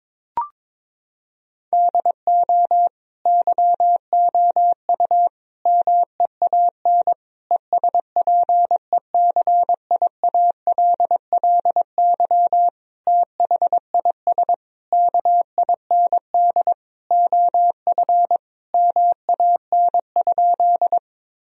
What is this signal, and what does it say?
Morse code, heard as a keyed tone, about 22 wpm, 700 Hz. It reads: DO YOU MEAN ESPECIALLY THIS KIND OF MAN?
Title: Sentences from Top 1000 Words - Review Part 4 - 22wpm